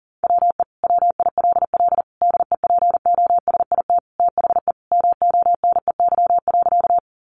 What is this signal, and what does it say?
Morse code, heard as a keyed tone, about 40 wpm, 700 Hz. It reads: WE WILL DEPOSIT THE MONEY.
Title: Sentences from Top 1300 Words 40wpm